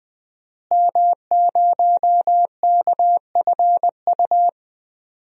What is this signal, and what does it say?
Morse code, heard as a keyed tone, about 20 wpm, 700 Hz. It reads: M0KFU
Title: DXCC-100 Call Signs - 5 Characters - Repeated 2x - 20wpm